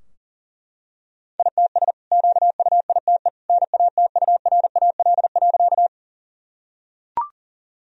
Morse code, heard as a keyed tone, about 40 wpm, 700 Hz. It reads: ITS QUITE NATURAL.
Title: Sentences from Top 1000 Words - Review Part 4 - 40wpm